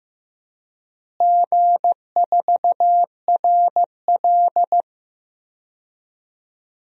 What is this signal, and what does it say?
Morse code, heard as a keyed tone, about 16 wpm, 700 Hz. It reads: G4RL
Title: DXCC-100 Call Signs - 4 Characters - Repeated 3x - 15wpm